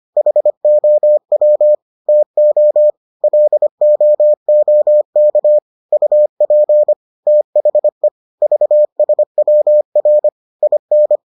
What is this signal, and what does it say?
Morse code, heard as a keyed tone, about 25 wpm, 600 Hz. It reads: HOW TO LOOK UP THE VSWR IN